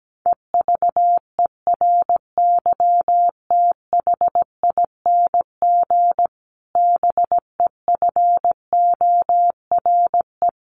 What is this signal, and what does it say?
Morse code, heard as a keyed tone, about 17 wpm, 700 Hz. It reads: EVERYTHING BEFORE
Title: Sentences from Top 1500 Words 17wpm